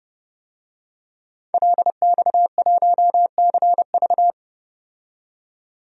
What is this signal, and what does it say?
Morse code, heard as a keyed tone, about 30 wpm, 700 Hz. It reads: LX1CV
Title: DXCC-100 Call Signs - 5 Characters - 30wpm